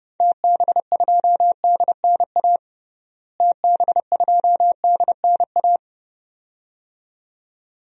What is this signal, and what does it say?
Morse code, heard as a keyed tone, about 30 wpm, 700 Hz. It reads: TB2DNA TB2DNA
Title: DXCC-100 Call Signs - 6 Characters - Repeated 2x - 30wpm